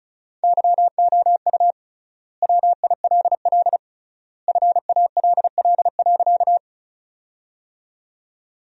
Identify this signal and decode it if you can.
Morse code, heard as a keyed tone, about 36 wpm, 700 Hz. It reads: YOU WILL FALL.
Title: 3x Word-Spacing - Sentences from Top 500 Words - 35wpm